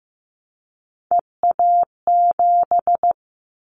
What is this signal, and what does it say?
Morse code, heard as a keyed tone, about 15 wpm, 700 Hz. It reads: EA7
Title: Callsign Prefixes for Top 25 DXCC - 15wpm